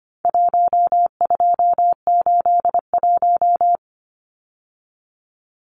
Morse code, heard as a keyed tone, about 25 wpm, 700 Hz. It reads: 1281
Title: Numbers - 4-Digit 25wpm